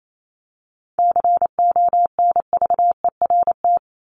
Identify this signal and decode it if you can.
Morse code, heard as a keyed tone, about 28 wpm, 700 Hz. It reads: CONVERT
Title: Top 1751-2000 Words 28wpm